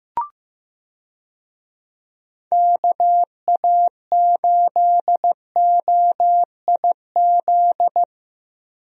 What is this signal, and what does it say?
Morse code, heard as a keyed tone, about 15 wpm, 700 Hz. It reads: KA8OIZ
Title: USA Call Signs - 6 Characters - Repeated 3x - 15wpm